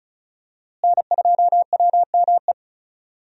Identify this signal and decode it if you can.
Morse code, heard as a keyed tone, about 36 wpm, 700 Hz. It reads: N2WME
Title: USA Call Signs - 5 Characters - Repeated 2x - 35wpm